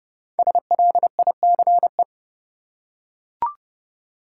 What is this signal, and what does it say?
Morse code, heard as a keyed tone, about 30 wpm, 700 Hz. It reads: SLICE
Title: Five Letter Words - 30wpm